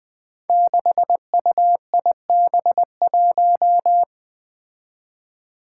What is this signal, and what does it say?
Morse code, heard as a keyed tone, about 20 wpm, 700 Hz. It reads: 6UIB1
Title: Five Challenging Letter-Number Combinations 20wpm